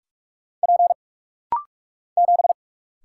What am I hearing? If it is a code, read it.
Morse code, heard as a keyed tone, about 44 wpm, 700 Hz. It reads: P 7
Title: Single Letter-Number - Rapid-Fire 45wpm